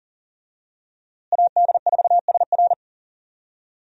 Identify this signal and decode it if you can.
Morse code, heard as a keyed tone, about 40 wpm, 700 Hz. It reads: AD4SR
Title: USA Call Signs - 5 Characters - Repeated 2x - 40wpm